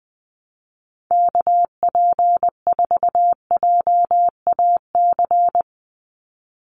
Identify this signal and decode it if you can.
Morse code, heard as a keyed tone, about 20 wpm, 700 Hz. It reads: KP4JAC
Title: USA Call Signs 20wpm